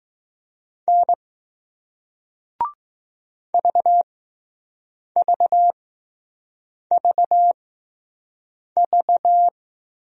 Morse code, heard as a keyed tone, about 20 wpm, spaced as for 6 wpm, 700 Hz. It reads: N VVVV